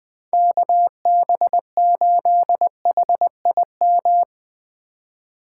Morse code, heard as a keyed tone, about 20 wpm, 700 Hz. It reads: KB8HIM
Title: USA Call Signs - 2x Character Spacing - 20wpm